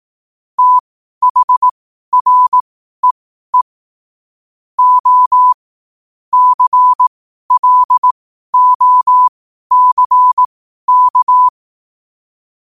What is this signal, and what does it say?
Morse code, heard as a keyed tone, about 18 wpm, spaced as for 12 wpm, 1000 Hz. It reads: THREE O CLOCK
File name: qotdfiles / 12wpm